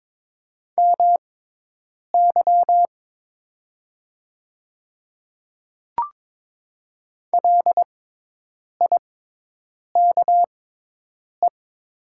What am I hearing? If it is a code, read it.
Morse code, heard as a keyed tone, about 22 wpm, spaced as for 8 wpm, 700 Hz. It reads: MY LIKE
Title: 6x Character Spacing - Top 100 Words - 22wpm